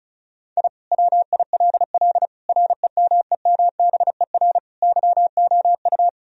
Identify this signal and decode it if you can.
Morse code, heard as a keyed tone, about 36 wpm, 700 Hz. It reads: I WILL REMEMBER YOU
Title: Sentences from Top 1000 Words - Review Part 2 - 35wpm